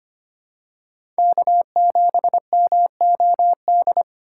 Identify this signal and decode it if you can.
Morse code, heard as a keyed tone, about 25 wpm, 700 Hz. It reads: K7MOD
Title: USA Call Signs - 5 Characters - 6x Character Spacing - 25wpm